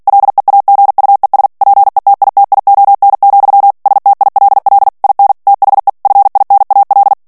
Morse code, heard as a keyed tone, about 48 wpm, 800 Hz. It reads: LEAGUES PETITION, STILL IN THE FINAL
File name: ANGLAIS texte clair 3 vitesse 45 mots minute